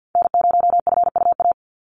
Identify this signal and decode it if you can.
Morse code, heard as a keyed tone, about 49 wpm, 700 Hz. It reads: N0FUA
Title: USA Call Signs 50wpm